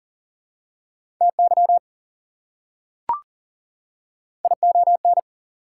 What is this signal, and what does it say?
Morse code, heard as a keyed tone, about 40 wpm, 700 Hz. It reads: TY ION
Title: Prefixes and Suffixes 40wpm